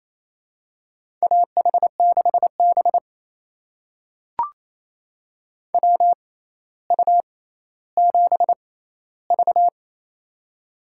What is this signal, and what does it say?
Morse code, heard as a keyed tone, about 28 wpm, spaced as for 14 wpm, 700 Hz. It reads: AH6B WU7V
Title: USA Call Signs - 4 Characters - 6x Character Spacing - 28wpm